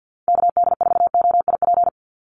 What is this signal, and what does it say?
Morse code, heard as a keyed tone, about 50 wpm, 700 Hz. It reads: KD4OIP
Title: USA Call Signs 50wpm